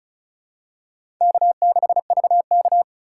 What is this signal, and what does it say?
Morse code, heard as a keyed tone, about 34 wpm, 700 Hz. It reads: K6VK
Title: USA Call Signs - 4 Characters - 6x Character Spacing - 35wpm